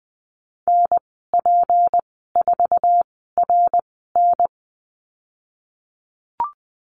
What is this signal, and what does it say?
Morse code, heard as a keyed tone, about 20 wpm, spaced as for 15 wpm, 700 Hz. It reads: NP4RN
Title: 167 Lesson 27g - Number 7 - Call Signs 20wpm